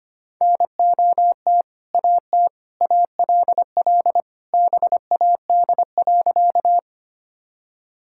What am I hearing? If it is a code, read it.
Morse code, heard as a keyed tone, about 25 wpm, 700 Hz. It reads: NOT AT ALL BAD.
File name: practice / 25wpm / Sentences with 2-Word Phrases - 25wpm